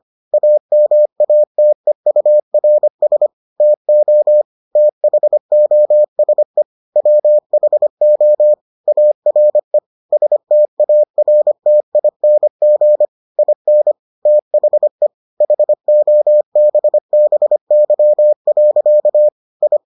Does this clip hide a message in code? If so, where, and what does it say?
Yes: Morse code from 0.3 to 19.8 s, heard as a keyed tone, about 25 wpm, 600 Hz. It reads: AMATEURS TO THOSE WHO ARE STARTING IN THE HOBBY. I